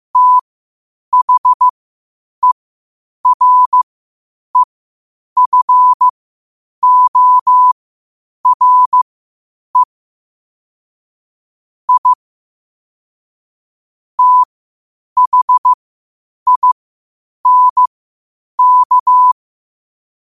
Morse code, heard as a keyed tone, about 15 wpm, spaced as for 8 wpm, 1000 Hz. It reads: THEREFORE I THINK
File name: qotdfiles / 8wpm